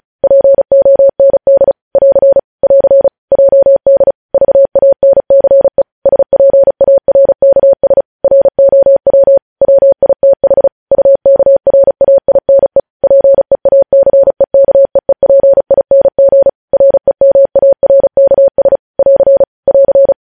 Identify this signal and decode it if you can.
Morse code, heard as a keyed tone, about 36 wpm, 550 Hz. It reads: POND <AR> <AR> JD VANCE SPARKS ROW WITH UKRAINE PEACEKEEPING REMARKS <AR> <AR>
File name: podcast / News headlines-1x-35x35